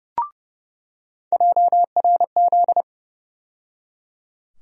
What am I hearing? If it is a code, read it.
Morse code, heard as a keyed tone, about 32 wpm, 700 Hz. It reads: JRZ
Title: USA Call Sign Postfixes - 3 Letters 30wpm